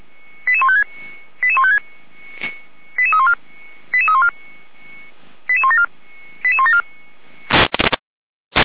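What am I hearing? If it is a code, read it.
Five-tone selective call (ZVEI), heard as quick runs of five tones, tones of about 70 ms. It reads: 88156 88156 88314 88314 88174 88174
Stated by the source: ZVEI five-tone selective call received on SDR.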